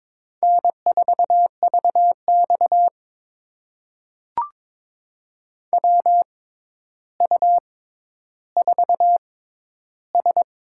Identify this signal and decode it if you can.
Morse code, heard as a keyed tone, about 22 wpm, spaced as for 11 wpm, 700 Hz. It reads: N4VX WU4S
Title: USA Call Signs - 4 Characters - 6x Character Spacing - 22wpm